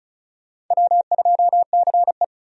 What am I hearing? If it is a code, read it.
Morse code, heard as a keyed tone, about 34 wpm, 700 Hz. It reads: W2CE
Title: USA Call Signs - 4 Characters - 35wpm